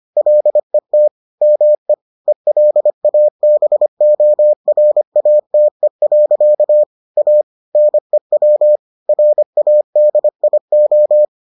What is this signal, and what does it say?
Morse code, heard as a keyed tone, about 25 wpm, 600 Hz. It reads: LET ME ELABORATE. A NEW RADIO